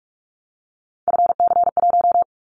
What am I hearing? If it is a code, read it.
Morse code, heard as a keyed tone, about 44 wpm, 700 Hz. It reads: FC1
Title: Callsign Prefixes for Top 25 DXCC - Repeated 3x - 45wpm